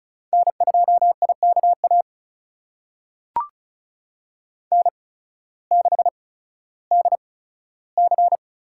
Morse code, heard as a keyed tone, about 34 wpm, spaced as for 16 wpm, 700 Hz. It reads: N2IKA N6DC
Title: USA Call Signs - 5 Characters - 8x Character Spacing - 35wpm